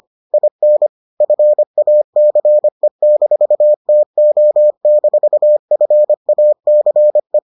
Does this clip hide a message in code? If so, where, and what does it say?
Yes: Morse code from 0.3 to 7.4 s, heard as a keyed tone, about 25 wpm, 600 Hz. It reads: IN FACE-TO-FACE